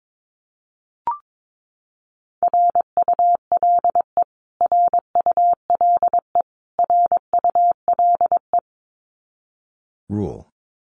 Morse code, heard as a keyed tone, about 22 wpm, 700 Hz. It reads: RULE RULE RULE
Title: Top 500 Words Repeated 22wpm